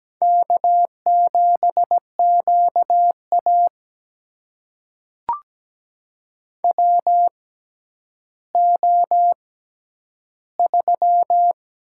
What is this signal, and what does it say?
Morse code, heard as a keyed tone, about 17 wpm, spaced as for 10 wpm, 700 Hz. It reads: K7QA WO3
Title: USA Call Signs - 4 Characters - 6x Character Spacing - 17wpm